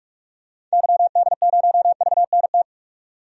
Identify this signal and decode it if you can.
Morse code, heard as a keyed tone, about 46 wpm, 700 Hz. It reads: YD0VNT
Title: DXCC-100 Call Signs - 6 Characters - 45wpm